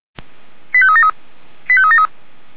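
Five-tone selective call (ZVEI), heard as quick runs of five tones, tones of about 70 ms. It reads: 85373 85373
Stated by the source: ZVEI five-tone call received on SDR, fire brigade Feuerwehr Orlamünde.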